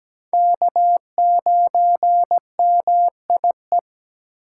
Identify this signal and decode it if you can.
Morse code, heard as a keyed tone, about 17 wpm, 700 Hz. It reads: K9MIE
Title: USA Call Signs - 4x Character Spacing - 17wpm